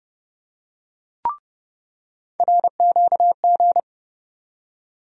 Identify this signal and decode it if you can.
Morse code, heard as a keyed tone, about 30 wpm, 700 Hz. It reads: RQG
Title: USA Call Sign Postfixes - 3 Letters 30wpm